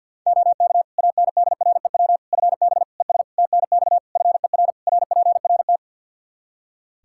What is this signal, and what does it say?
Morse code, heard as a keyed tone, about 50 wpm, 700 Hz. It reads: OK ANDREW FB ES TNX FER RPRT
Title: Practice QSO - Exchange 2 - Info - 50wpm